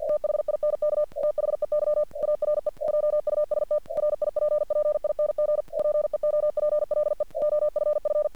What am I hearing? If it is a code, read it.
Morse code, heard as a keyed tone, about 48 wpm, 600 Hz. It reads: THINK THEY ARE JUST RIPPING PEOPLE OFF
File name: archived / CRSnet